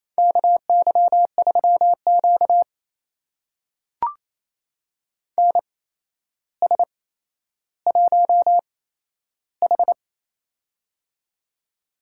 Morse code, heard as a keyed tone, about 28 wpm, spaced as for 12 wpm, 700 Hz. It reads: KY3Q NS1H
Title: USA Call Signs - 4 Characters - 8x Character Spacing - 28wpm